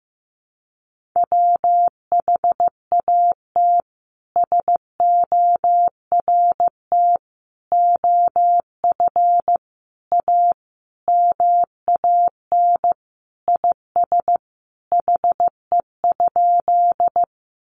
Morse code, heard as a keyed tone, about 15 wpm, 700 Hz. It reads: WHAT SORT OF A MAN IS HE?
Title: Sentences from Top 1300 Words 15wpm